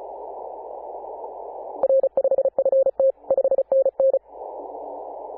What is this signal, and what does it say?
Morse code, heard as a keyed tone, about 34 wpm, 540 Hz. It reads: R5FT 5NN
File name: Apr/29 2042z JD1BLY 14.011MHz CW